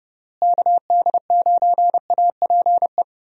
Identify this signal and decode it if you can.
Morse code, heard as a keyed tone, about 30 wpm, 700 Hz. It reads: KD9APE
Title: USA Call Signs - 6 Characters - Repeated 2x - 30wpm